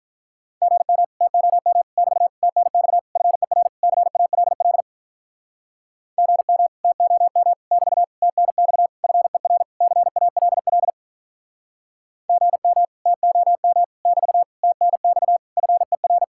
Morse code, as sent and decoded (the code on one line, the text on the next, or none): --. -- / - --- -- / -...- / - -. -..- / ..-. .-. / -.-. .- .-.. .-.. / --. -- / - --- -- / -...- / - -. -..- / ..-. .-. / -.-. .- .-.. .-.. / --. -- / - --- -- / -...- / - -. -..- / ..-. .-.
GM TOM <BT> TNX FER CALL GM TOM <BT> TNX FER CALL GM TOM <BT> TNX FER